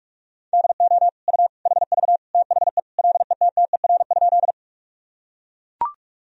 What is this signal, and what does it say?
Morse code, heard as a keyed tone, about 41 wpm, 700 Hz. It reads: DO U HV THE LETTER?
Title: Sentences from Top 300 Words and Abbreviations 45wpm